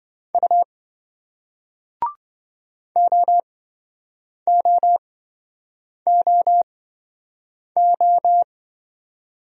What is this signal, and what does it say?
Morse code, heard as a keyed tone, about 26 wpm, spaced as for 7 wpm, 700 Hz. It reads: U OOOO